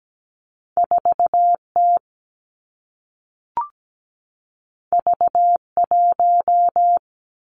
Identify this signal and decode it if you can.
Morse code, heard as a keyed tone, about 17 wpm, 700 Hz. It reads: 4T V1
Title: Two Challenging Letter-Number Combinations 17wpm